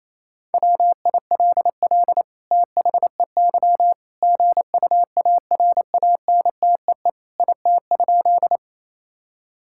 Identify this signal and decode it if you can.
Morse code, heard as a keyed tone, about 28 wpm, 700 Hz. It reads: WILL THEY GUARANTEE IT?
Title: Sentences from Top 1600 Words 28wpm